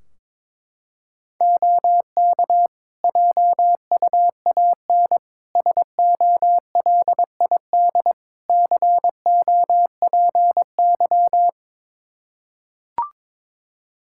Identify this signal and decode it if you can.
Morse code, heard as a keyed tone, about 22 wpm, 700 Hz. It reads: OK JUAN SOLID COPY
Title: Practice QSO - Exchange 3 - More Info - 22wpm